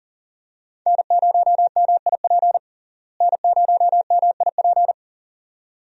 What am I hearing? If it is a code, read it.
Morse code, heard as a keyed tone, about 40 wpm, 700 Hz. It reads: N0MIP N0MIP
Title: USA Call Signs - 5 Characters - Repeated 2x - 40wpm